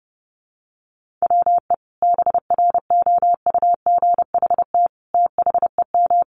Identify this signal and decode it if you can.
Morse code, heard as a keyed tone, about 30 wpm, 700 Hz. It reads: WE BROUGHT THEM
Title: Sentences from Top 500 Words 30wpm